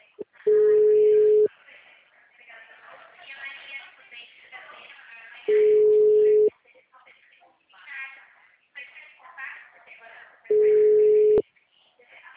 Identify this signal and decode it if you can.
Ringback tone (Europe, 425 Hz), with speech.
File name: out- -2005- - - .20439